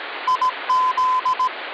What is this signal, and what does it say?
Morse code (Russian alphabet), heard as a keyed tone, about 18 wpm, 1020 Hz, with noise. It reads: ИЗ
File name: original